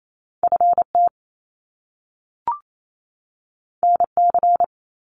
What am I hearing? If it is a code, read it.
Morse code, heard as a keyed tone, about 28 wpm, 700 Hz. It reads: FT NC
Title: Top Two Letter Combinations in Words 28wpm